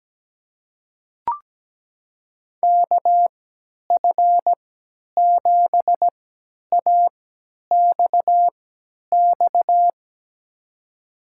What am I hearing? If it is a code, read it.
Morse code, heard as a keyed tone, about 17 wpm, spaced as for 10 wpm, 700 Hz. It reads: KF7AXX
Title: USA Call Signs - 6 Characters - 3x Character Spacing - 17wpm